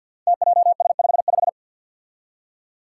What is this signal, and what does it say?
Morse code, heard as a keyed tone, about 47 wpm, 700 Hz. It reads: TJS55
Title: Five Challenging Letter-Number Combinations 50wpm